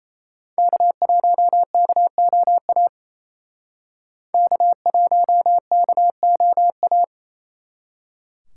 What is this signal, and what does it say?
Morse code, heard as a keyed tone, about 34 wpm, 700 Hz. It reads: K1KOA K1KOA
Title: Speed Racing - USA 1x3 Call Signs 28wpm